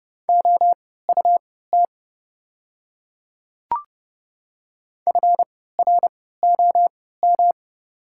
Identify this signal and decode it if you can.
Morse code, heard as a keyed tone, about 30 wpm, spaced as for 17 wpm, 700 Hz. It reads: OUT FROM
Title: 3x Character Spacing - Top 100 Words - 30wpm